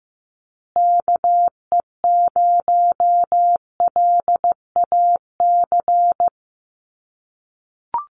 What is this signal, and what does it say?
Morse code, heard as a keyed tone, about 15 wpm, 700 Hz. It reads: KE0LAC
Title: USA Call Signs Repeated 3x 15wpm